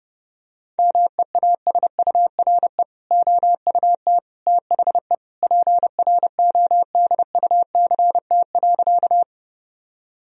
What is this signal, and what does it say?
Morse code, heard as a keyed tone, about 32 wpm, 700 Hz. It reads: MEASURE OUT THE PRODUCT.
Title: Sentences from Top 1000 Words - Review Part 2 - 30wpm